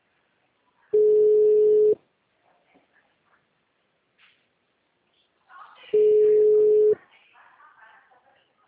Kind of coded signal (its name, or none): telephone call progress tone